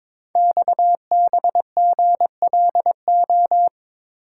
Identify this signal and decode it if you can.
Morse code, heard as a keyed tone, about 22 wpm, 700 Hz. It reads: XBGLO
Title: Code Groups - 5 Letter:Numbers - 22wpm